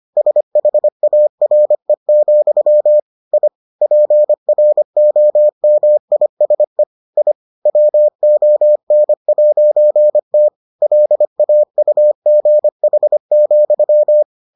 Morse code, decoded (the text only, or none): SHARE, I PROMISE I WON'T LAUGH,